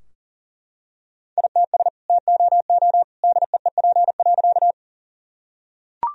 Morse code, heard as a keyed tone, about 40 wpm, 700 Hz. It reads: ITS TOO DEEP.
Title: Sentences from Top 1000 Words - Review Part 3 - 40wpm